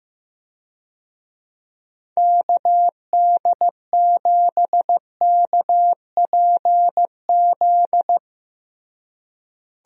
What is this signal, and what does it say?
Morse code, heard as a keyed tone, about 15 wpm, 700 Hz. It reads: KD7KPZ
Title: USA Call Signs - 6 Characters - Repeated 2x - 15wpm